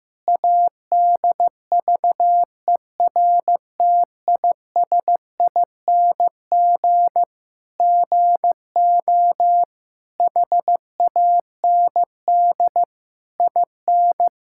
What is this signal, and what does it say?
Morse code, heard as a keyed tone, about 15 wpm, 700 Hz. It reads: ADVERTISING GO HAND IN
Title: Sentences from Top 1000 Words - Review Part 6 - 15wpm